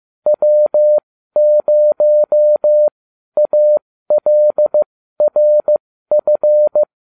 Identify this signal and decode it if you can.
Morse code, heard as a keyed tone, about 15 wpm, spaced as for 12 wpm, 600 Hz. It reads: W0ALRF